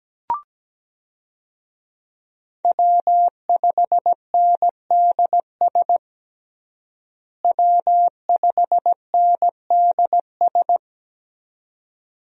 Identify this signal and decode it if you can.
Morse code, heard as a keyed tone, about 17 wpm, 700 Hz. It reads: W5NDS W5NDS
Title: USA Call Signs - 5 Characters - Repeated 2x - 17wpm